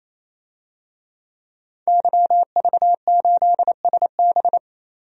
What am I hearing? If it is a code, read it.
Morse code, heard as a keyed tone, about 28 wpm, 700 Hz. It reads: YV8SB
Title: DXCC-100 Call Signs - 5 Characters - 28wpm